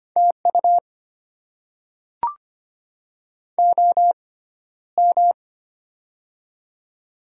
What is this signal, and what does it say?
Morse code, heard as a keyed tone, about 25 wpm, spaced as for 13 wpm, 690 Hz. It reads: TU OM
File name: Top 90 QSO Elements - 6x Character Spacing - 25wpm